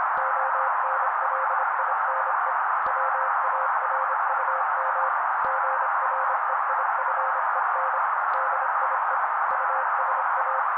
Morse code, heard as a keyed tone, about 25 wpm, 530 Hz. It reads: ONLINE WARUM GREIFEN DIE USA
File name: daily / 25wpm noise